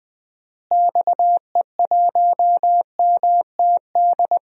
Morse code, heard as a keyed tone, about 20 wpm, 700 Hz. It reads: XE1MTD
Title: DXCC-100 Call Signs - 6 Characters - Repeated 2x - 20wpm